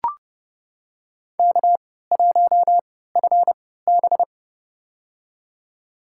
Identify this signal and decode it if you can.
Morse code, heard as a keyed tone, about 32 wpm, spaced as for 17 wpm, 700 Hz. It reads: K1FB
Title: USA Call Signs - 4 Characters - 3x Character Spacing - 30wpm